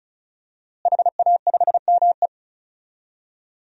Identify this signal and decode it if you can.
Morse code, heard as a keyed tone, about 36 wpm, 700 Hz. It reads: HA5ME
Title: DXCC-100 Call Signs - 5 Characters - Repeated 3x - 35wpm